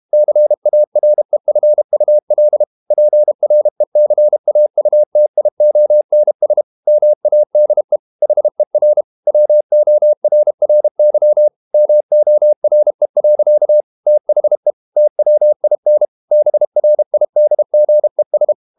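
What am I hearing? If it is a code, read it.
Morse code, heard as a keyed tone, about 32 wpm, 600 Hz. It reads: CAREFUL PRECAUTIONS MADE HER WORRY MORE. THE TWIN BRIDGES
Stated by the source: Morse code practice at 32 wpm from CW med Gnister, reading Ken Follett's novel Never in English.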